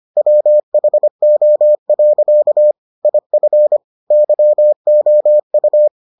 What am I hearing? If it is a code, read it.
Morse code, heard as a keyed tone, about 25 wpm, 600 Hz. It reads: WHO. IF YOU